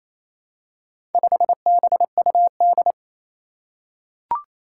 Morse code, heard as a keyed tone, about 27 wpm, 700 Hz. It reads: 5BUD
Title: Four Challenging Letter-Number Combinations 28wpm